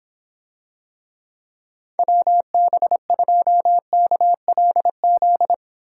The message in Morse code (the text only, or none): WB2KLZ